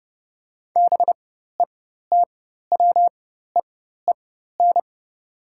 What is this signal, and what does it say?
Morse code, heard as a keyed tone, about 31 wpm, spaced as for 14 wpm, 700 Hz. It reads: BETWEEN
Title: 4x Character Spacing - Top 300 Words - 30wpm